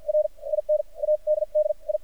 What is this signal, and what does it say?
Morse code, heard as a keyed tone, about 55 wpm, spaced as for 41 wpm, 610 Hz. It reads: R IN ADDI
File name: archived / CRSnet